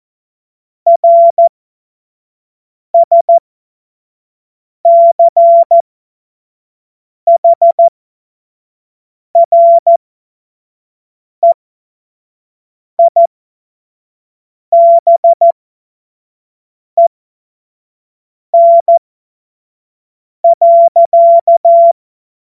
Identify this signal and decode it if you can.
Morse code, heard as a keyed tone, about 14 wpm, spaced as for 5 wpm, 680 Hz. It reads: RSCHREIBEN.
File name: Morsekurs / Witze / Witz 16 April 2024 25 F70